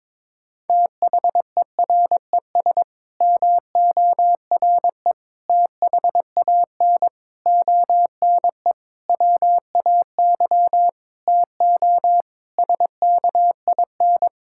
Morse code, heard as a keyed tone, about 22 wpm, 700 Hz. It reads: THERES MORE THAN ONE WAY TO SKIN
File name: Sets of 9 Words - 22wpm